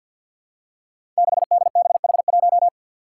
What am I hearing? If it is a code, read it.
Morse code, heard as a keyed tone, about 53 wpm, 700 Hz. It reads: 6DBH1
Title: Five Challenging Letter-Number Combinations 50wpm